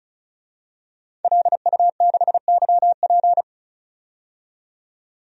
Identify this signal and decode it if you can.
Morse code, heard as a keyed tone, about 36 wpm, 700 Hz. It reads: LU6YP
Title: DXCC-100 Call Signs - 5 Characters - 35wpm